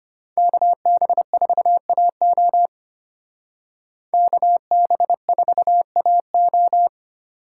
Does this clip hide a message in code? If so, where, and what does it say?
Yes: Morse code from 0.4 to 6.9 s, heard as a keyed tone, about 32 wpm, 700 Hz. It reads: KB4AO KB4AO